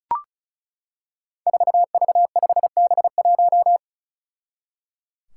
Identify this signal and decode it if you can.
Morse code, heard as a keyed tone, about 36 wpm, 700 Hz. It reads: 4V5B1
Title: Five Challenging Letter-Number Combinations 35wpm